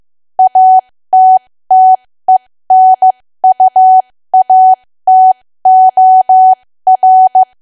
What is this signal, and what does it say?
Morse code, heard as a keyed tone, about 15 wpm, spaced as for 13 wpm, 750 Hz. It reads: ATTENUATOR
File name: cwa / W1AW-Level2 / 13WPM